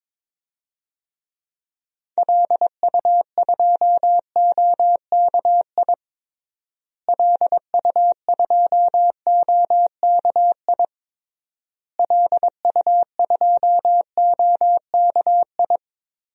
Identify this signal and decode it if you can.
Morse code, heard as a keyed tone, about 22 wpm, 700 Hz. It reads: LU2OKI LU2OKI LU2OKI
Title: DXCC-100 Call Signs - 6 Characters - Repeated 3x - 22wpm